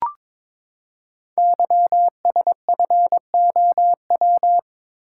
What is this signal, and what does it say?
Morse code, heard as a keyed tone, about 23 wpm, 700 Hz. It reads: YSFOW
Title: Code Groups - 5 Letter:Numbers - 22wpm